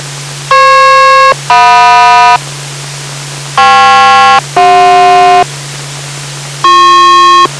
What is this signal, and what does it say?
Aircraft SELCAL, heard as two pairs of tones. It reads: FP-JM KP-CJ